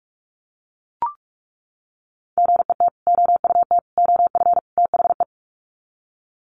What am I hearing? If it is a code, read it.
Morse code, heard as a keyed tone, about 44 wpm, 700 Hz. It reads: GET OUT OF THE
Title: Sets of 4 Words 45wpm